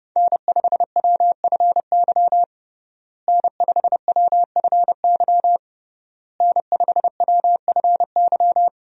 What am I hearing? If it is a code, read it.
Morse code, heard as a keyed tone, about 31 wpm, 700 Hz. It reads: N5WFY N5WFY N5WFY
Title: USA Call Signs - 5 Characters - Repeated 3x - 30wpm